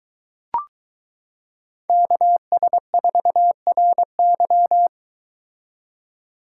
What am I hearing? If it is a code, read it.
Morse code, heard as a keyed tone, about 22 wpm, 700 Hz. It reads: KS4RY